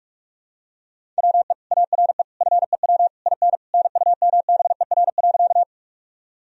Morse code, heard as a keyed tone, about 41 wpm, 700 Hz. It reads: WE ARE FEW IN NUMBER.
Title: Sentences from Top 1000 Words - Review Part 1 - 45wpm